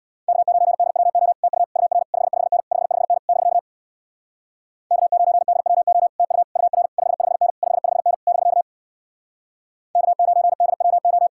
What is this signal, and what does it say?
Morse code, heard as a keyed tone, about 61 wpm, 700 Hz. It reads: K9DWO TU UR 55N 55N <BK> K9DWO TU UR 55N 55N <BK> K9DWO
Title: Speed-Racing Common QSO Phrases - 50wpm